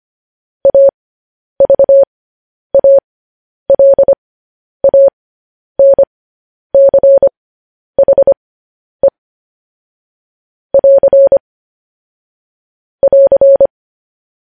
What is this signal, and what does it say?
Morse code, heard as a keyed tone, about 25 wpm, spaced as for 10 wpm, 550 Hz. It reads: AVALANCHE <AR> <AR>
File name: podcast / News headlines-1x-25x10